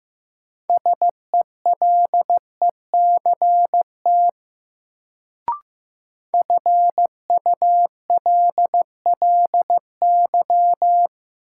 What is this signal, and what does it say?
Morse code, heard as a keyed tone, about 15 wpm, 700 Hz. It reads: SELECT FULLY
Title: Sentences from Top 1300 Words - New Word Drill - 15wpm